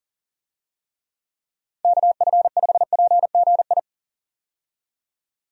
Morse code, heard as a keyed tone, about 40 wpm, 700 Hz. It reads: KF5PGI
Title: USA Call Signs - 6 Characters - 40wpm